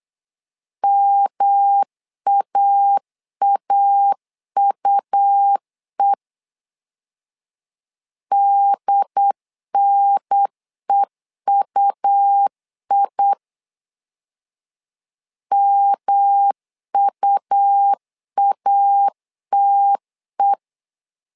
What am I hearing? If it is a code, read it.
Morse code, heard as a keyed tone, about 8 wpm, 800 Hz. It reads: MAAUE DNEUI MUATE
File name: LEZIONI / 3 MWRO / 3 M 8wpm 2300